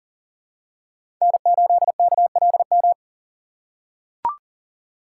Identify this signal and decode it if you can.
Morse code, heard as a keyed tone, about 40 wpm, 690 Hz. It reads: N8KLM
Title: USA Call Signs - 5 Characters - 6x Character Spacing - 40wpm